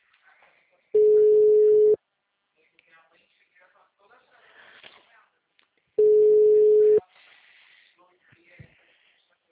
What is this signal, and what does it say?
Ringback tone (Europe, 425 Hz), with speech.
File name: out- -2013- - - .51